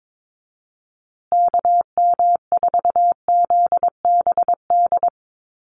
Morse code, heard as a keyed tone, about 22 wpm, 700 Hz. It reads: KM4ZBD
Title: USA Call Signs Repeated 3x 22wpm